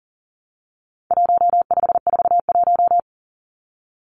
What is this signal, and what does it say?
Morse code, heard as a keyed tone, about 40 wpm, 700 Hz. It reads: 1541
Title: Numbers - 4-Digit 40wpm